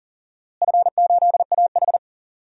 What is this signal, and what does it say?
Morse code, heard as a keyed tone, about 40 wpm, 700 Hz. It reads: F8AH
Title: DXCC-100 Call Signs - 4 Characters - Repeated 3x - 40wpm